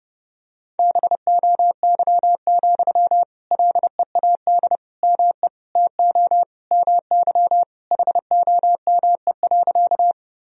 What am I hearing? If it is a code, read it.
Morse code, heard as a keyed tone, about 32 wpm, 700 Hz. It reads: BOY, LEAD ME TO MY HOME.
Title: Sentences from Top 1000 Words - Review Part 2 - 30wpm